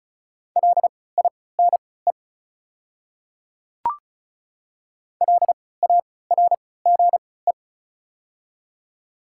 Morse code, heard as a keyed tone, about 34 wpm, spaced as for 20 wpm, 690 Hz. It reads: LINE LARGE
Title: 3x Character Spacing - Top 300 Words - 35wpm